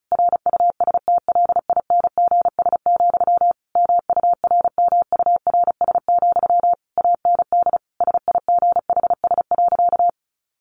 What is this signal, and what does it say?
Morse code, heard as a keyed tone, about 34 wpm, 700 Hz. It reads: RUSTLINGS, MURMURS, AND SIGHS.
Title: A Sound Of Thunder 35wpm